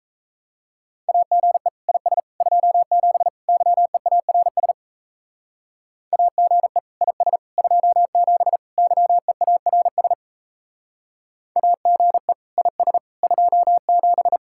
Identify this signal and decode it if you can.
Morse code, heard as a keyed tone, about 33 wpm, 700 Hz. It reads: AGE IS 27 YEARS AGE IS 27 YEARS AGE IS 27
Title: Speed-Racing Common QSO Phrases - 28wpm